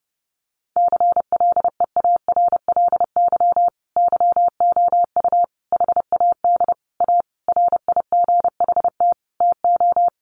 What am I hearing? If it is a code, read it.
Morse code, heard as a keyed tone, about 28 wpm, 700 Hz. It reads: CLEARLY YOU HAD A RIGHT TO
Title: Sentences from Top 1300 Words 30wpm